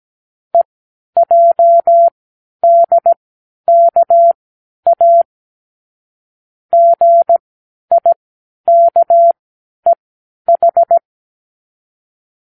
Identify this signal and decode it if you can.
Morse code, heard as a keyed tone, about 17 wpm, spaced as for 10 wpm, 680 Hz. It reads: EJDKA GIKEH
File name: k7qo / Lesson18(A-L)